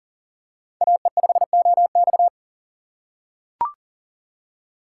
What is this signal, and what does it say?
Morse code, heard as a keyed tone, about 40 wpm, 700 Hz. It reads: AE5OX
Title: USA Call Signs - 5 Characters - 4x Character Spacing - 40wpm